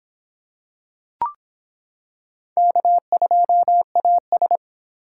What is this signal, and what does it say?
Morse code, heard as a keyed tone, about 26 wpm, 700 Hz. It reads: K2AS